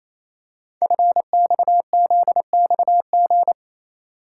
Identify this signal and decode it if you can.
Morse code, heard as a keyed tone, about 27 wpm, 700 Hz. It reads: FXZXG
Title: Code Groups - 5 Letters - 28wpm